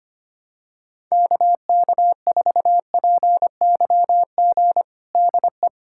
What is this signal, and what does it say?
Morse code, heard as a keyed tone, about 25 wpm, 700 Hz. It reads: KK4PYG DE
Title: USA Callsign DE Callsign 25wpm